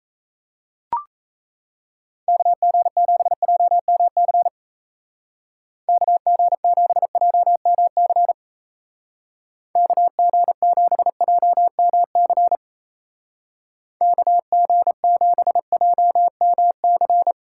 Morse code, heard as a keyed tone, about 34 wpm, 700 Hz. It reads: KG7JMC KG7JMC KG7JMC KG7JMC
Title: Speed Racing - USA 2x3 Call Signs 28wpm